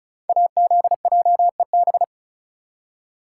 Morse code, heard as a keyed tone, about 36 wpm, 700 Hz. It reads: AZJEB
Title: Code Groups - 5 Letter:Numbers - 35wpm